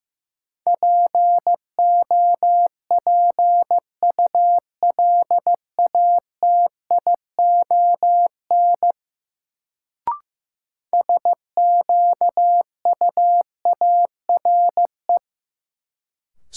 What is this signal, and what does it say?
Morse code, heard as a keyed tone, about 15 wpm, 700 Hz. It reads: POPULATION SQUARE
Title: Sentences from Top 1000 Words - Review Part 7 - New Word Drill - 15wpm